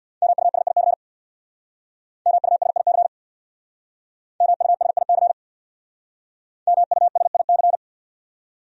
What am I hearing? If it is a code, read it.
Morse code, heard as a keyed tone, about 62 wpm, 700 Hz. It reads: MUSIC MUSIC MUSIC MUSIC
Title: Speed-Racing Top 300 Words 50wpm